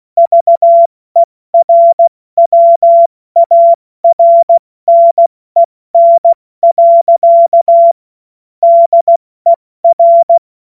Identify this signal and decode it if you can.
Morse code, heard as a keyed tone, about 16 wpm, 680 Hz. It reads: VERWARNEN. DER
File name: Morsekurs / Witze / Witz 18. Januar 2022, 70 F80